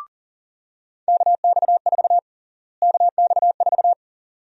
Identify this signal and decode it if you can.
Morse code, heard as a keyed tone, about 40 wpm, 700 Hz. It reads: KX4 KX4
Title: USA Call Sign Prefixes - Repeated 2x - 40wpm